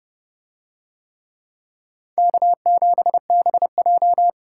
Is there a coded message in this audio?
Yes: Morse code from 2.2 to 4.3 s, heard as a keyed tone, about 31 wpm, 700 Hz. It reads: K7BJ